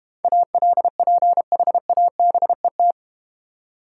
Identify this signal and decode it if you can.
Morse code, heard as a keyed tone, about 31 wpm, 690 Hz. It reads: ALPHABET
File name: Easy Words - Speed-Racing with Hints - 25wpm